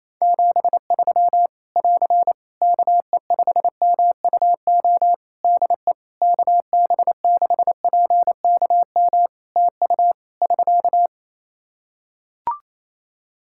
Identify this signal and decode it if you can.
Morse code, heard as a keyed tone, about 28 wpm, 700 Hz. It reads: 73 <AR> KE5MUO DE KB6PKM TU <SK>
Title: Practice QSO - Exchange 4 - Ending - 28wpm